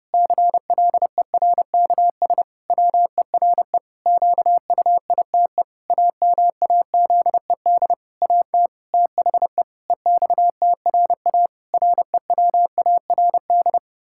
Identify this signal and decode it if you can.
Morse code, heard as a keyed tone, about 30 wpm, 700 Hz. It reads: CLERKS WERE QUITE AMAZED AT THE EXTRA REWARD
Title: Pangram Challenge v2 - 30wpm